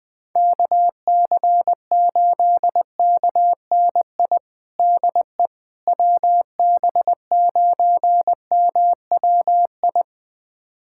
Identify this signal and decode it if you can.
Morse code, heard as a keyed tone, about 21 wpm, 700 Hz. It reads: KC8KNI DE WB9MWI
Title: USA Callsign DE Callsign 20wpm